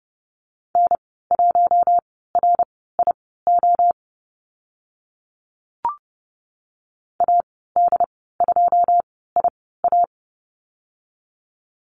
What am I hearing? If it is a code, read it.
Morse code, heard as a keyed tone, about 30 wpm, spaced as for 17 wpm, 700 Hz. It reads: N1RIO AD2IA
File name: Lesson 14g - Number 2 - Call Signs 30wpm